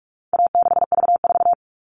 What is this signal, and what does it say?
Morse code, heard as a keyed tone, about 44 wpm, 700 Hz. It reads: A6V4
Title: All Cut Numbers - 4-Digit 45wpm